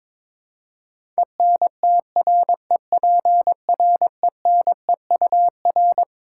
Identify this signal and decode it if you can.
Morse code, heard as a keyed tone, about 22 wpm, 700 Hz. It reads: ENTREPRENEUR
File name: Twelve Letter Words - 22wpm